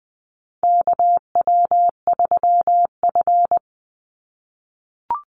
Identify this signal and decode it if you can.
Morse code, heard as a keyed tone, about 20 wpm, 700 Hz. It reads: KW3F
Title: Call Signs - 20wpm